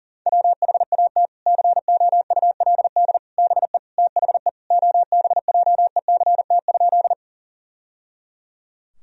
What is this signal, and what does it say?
Morse code, heard as a keyed tone, about 40 wpm, 700 Hz. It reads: WHAT COULD BE THE OBJECT?
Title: Sentences from Top 1000 Words - Review Part 3 - 40wpm